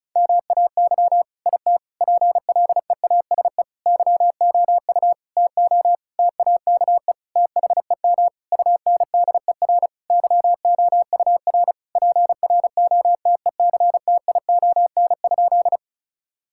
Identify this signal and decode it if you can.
Morse code, heard as a keyed tone, about 34 wpm, 700 Hz. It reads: MAY IT PLEASE YOU TO TAKE THEM UNDER YOUR PROTECTION?
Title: Sentences from Top 1600 Words 35wpm